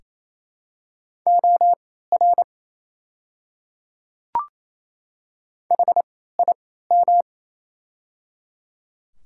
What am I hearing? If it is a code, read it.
Morse code, heard as a keyed tone, about 27 wpm, spaced as for 16 wpm, 690 Hz. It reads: OR HIM
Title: 3x Character Spacing - Top 200 Words - 28wpm